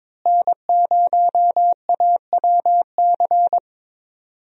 Morse code, heard as a keyed tone, about 21 wpm, 700 Hz. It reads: N0AWC
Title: USA Call Signs - 5 Characters - Repeated 2x - 22wpm